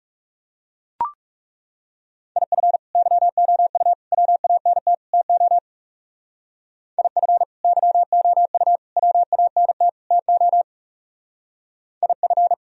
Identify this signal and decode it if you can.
Morse code, heard as a keyed tone, about 35 wpm, 700 Hz. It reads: IF YOU WANT TO IF YOU WANT TO IF
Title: Speed-Racing - Sets of 4 Words 30wpm